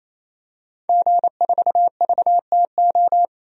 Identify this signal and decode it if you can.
Morse code, heard as a keyed tone, about 27 wpm, 700 Hz. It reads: G4VTO
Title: DXCC-100 Call Signs - 5 Characters - Repeated 3x - 28wpm